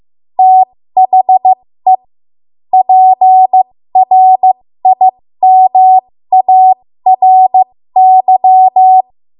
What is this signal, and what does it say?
Morse code, heard as a keyed tone, about 15 wpm, spaced as for 13 wpm, 750 Hz. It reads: THE PRIMARY
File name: Morse / 13WPM